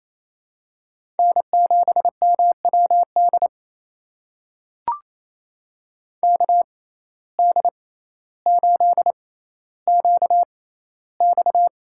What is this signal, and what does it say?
Morse code, heard as a keyed tone, about 28 wpm, spaced as for 14 wpm, 700 Hz. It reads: N7MWD KD8QX
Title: USA Call Signs - 6x Character Spacing - 28wpm